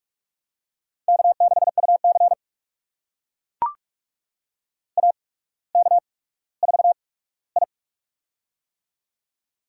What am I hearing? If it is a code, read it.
Morse code, heard as a keyed tone, about 44 wpm, spaced as for 19 wpm, 700 Hz. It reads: K6UC AK4I
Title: USA Call Signs - 4 Characters - 8x Character Spacing - 45wpm